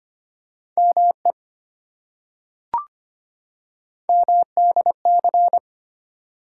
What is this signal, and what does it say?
Morse code, heard as a keyed tone, about 25 wpm, 700 Hz. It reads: ME MDC
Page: morse-code-ninja-practice-25wpm